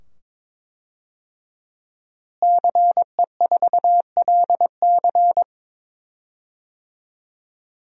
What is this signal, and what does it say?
Morse code, heard as a keyed tone, about 22 wpm, 700 Hz. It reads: CE4LC
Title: DXCC-100 Call Signs - 5 Characters - Repeated 2x - 22wpm